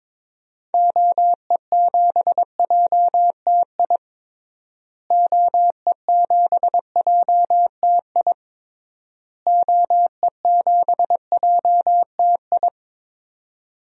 Morse code, heard as a keyed tone, about 22 wpm, 700 Hz. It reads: OE7JTI OE7JTI OE7JTI
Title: DXCC-100 Call Signs - 6 Characters - Repeated 3x - 22wpm